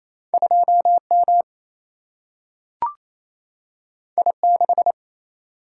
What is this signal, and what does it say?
Morse code, heard as a keyed tone, about 28 wpm, 700 Hz. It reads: 2 M I 6
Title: Single-Letter Single-Number 28wpm